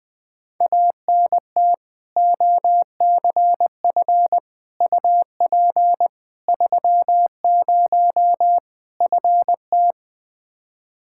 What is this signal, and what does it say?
Morse code, heard as a keyed tone, about 20 wpm, 700 Hz. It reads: ANT OCF UP 30 FT
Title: Practice QSO - Exchange 2 - Info - 20wpm